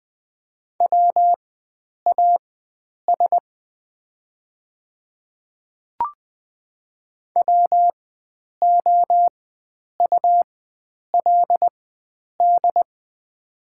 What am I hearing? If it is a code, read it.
Morse code, heard as a keyed tone, about 20 wpm, spaced as for 9 wpm, 690 Hz. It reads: WAS WOULD
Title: 4x Character Spacing - Top 300 Words - 20wpm